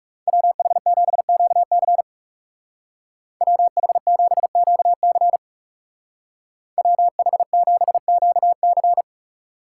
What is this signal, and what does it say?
Morse code, heard as a keyed tone, about 36 wpm, 700 Hz. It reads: WH7QC WH7QC WH7QC
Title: Speed Racing - USA 2x2 Call Signs 30wpm